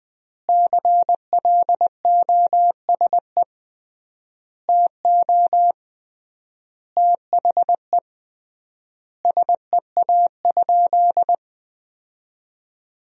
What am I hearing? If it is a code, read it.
Morse code, heard as a keyed tone, about 20 wpm, 700 Hz. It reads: CLOSE TO THE SEA?
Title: 3x Word-Spacing - Sentences from Top 300 Words - 20wpm